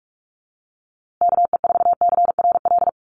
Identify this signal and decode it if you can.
Morse code, heard as a keyed tone, about 44 wpm, 700 Hz. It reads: KE4CRL
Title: USA Call Signs Repeated 3x 45wpm